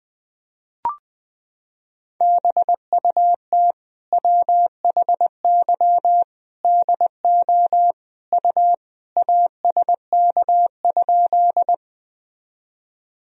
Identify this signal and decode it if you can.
Morse code, heard as a keyed tone, about 21 wpm, 700 Hz. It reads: BUT WHY DO U ASK?
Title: Sentences from Top 200 Words and Abbreviations 20wpm